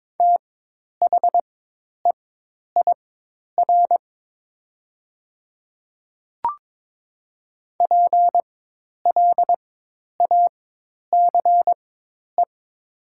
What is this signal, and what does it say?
Morse code, heard as a keyed tone, about 22 wpm, spaced as for 10 wpm, 690 Hz. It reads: THEIR PLACE
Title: 4x Character Spacing - Top 200 Words - 22wpm